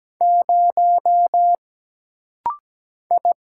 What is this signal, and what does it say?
Morse code, heard as a keyed tone, about 17 wpm, 700 Hz. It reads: 0 I
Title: Single Letter-Number - Rapid-Fire 17wpm